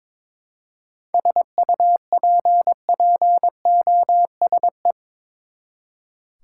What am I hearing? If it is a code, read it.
Morse code, heard as a keyed tone, about 22 wpm, 700 Hz. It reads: SUPPOSE
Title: Seven Letter Words - 22wpm